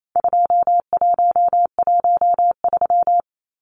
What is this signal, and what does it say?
Morse code, heard as a keyed tone, about 28 wpm, 700 Hz. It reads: 2113
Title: Numbers - 4-Digit 28wpm